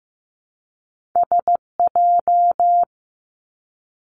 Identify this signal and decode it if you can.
Morse code, heard as a keyed tone, about 15 wpm, 700 Hz. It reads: SJ
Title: Two Challenging Letter-Number Combinations 15wpm